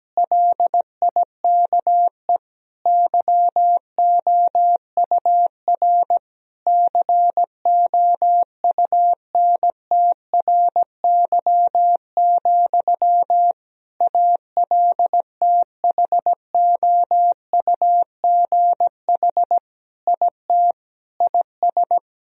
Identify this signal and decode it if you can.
Morse code, heard as a keyed tone, about 17 wpm, 700 Hz. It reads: LIKE YOUR COUNTRY, ALTHOUGH IT IS
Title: Sentences from Top 1000 Words - Review Part 3 - 17wpm